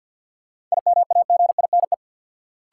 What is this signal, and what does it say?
Morse code, heard as a keyed tone, about 48 wpm, 700 Hz. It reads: IMAGINE
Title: Sentences from Top 1400 Words - New Word Drill - 50wpm